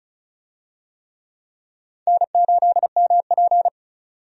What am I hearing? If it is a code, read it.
Morse code, heard as a keyed tone, about 36 wpm, 700 Hz. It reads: N8MP
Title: USA Call Signs - 4 Characters - 35wpm